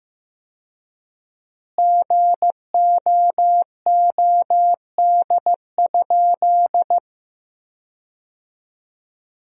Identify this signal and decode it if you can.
Morse code, heard as a keyed tone, about 15 wpm, 700 Hz. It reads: GOOD?
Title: 4x Word-Spacing - Sentences from Top 200 Words - 15wpm